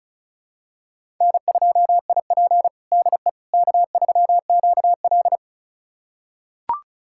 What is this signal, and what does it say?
Morse code, heard as a keyed tone, about 36 wpm, 700 Hz. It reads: N2IP DE K3QL
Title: USA Callsign DE Callsign - 4 Characters - 35wpm